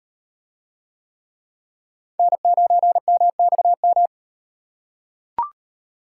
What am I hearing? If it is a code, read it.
Morse code, heard as a keyed tone, about 38 wpm, 700 Hz. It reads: N9MXM